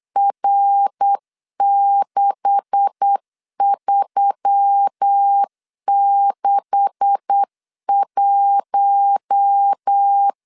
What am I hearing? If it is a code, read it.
Morse code, heard as a keyed tone, about 8 wpm, 800 Hz. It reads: R6361
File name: LEZIONI / 9 286 8wpm 3425